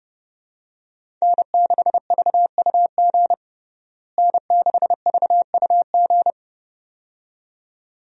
Morse code, heard as a keyed tone, about 32 wpm, 700 Hz. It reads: N6VUG N6VUG
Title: USA Call Signs - 5 Characters - Repeated 2x - 30wpm